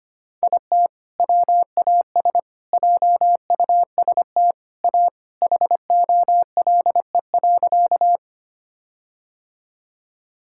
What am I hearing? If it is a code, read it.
Morse code, heard as a keyed tone, about 25 wpm, 700 Hz. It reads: IT WAS JUST A HOLE.
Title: Sentences from Top 1600 Words 25wpm